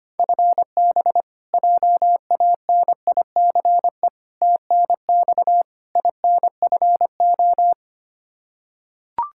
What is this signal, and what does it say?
Morse code, heard as a keyed tone, about 25 wpm, 700 Hz. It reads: FB JANICE TNX INFO
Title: K1USN SST QSO Practice 25wpm